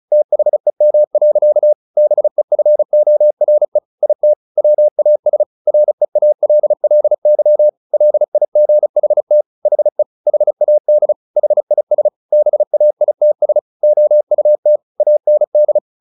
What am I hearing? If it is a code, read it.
Morse code, heard as a keyed tone, about 34 wpm, 600 Hz. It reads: THEM. BEFORE IT WAS REALLY LIGHT HE HAD HIS BAITS OUT AND
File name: the old man and the sea / the old man and the sea 1 35wpm